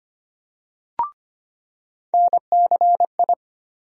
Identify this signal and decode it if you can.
Morse code, heard as a keyed tone, about 25 wpm, 700 Hz. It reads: NCI
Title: USA Call Sign Postfixes - 3 Letters 25wpm